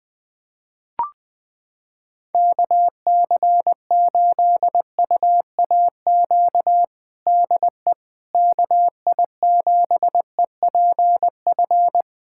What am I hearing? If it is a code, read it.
Morse code, heard as a keyed tone, about 20 wpm, 700 Hz. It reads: KC8UAQ DE KI7EPF
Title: USA Callsign DE Callsign - 6 Characters - 20wpm